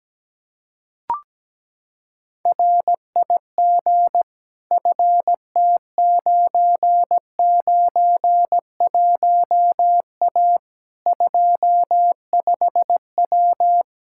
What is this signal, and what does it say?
Morse code, heard as a keyed tone, about 17 wpm, 700 Hz. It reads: RIG FT991A 25W
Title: Practice QSO - Exchange 2 - Info - 17wpm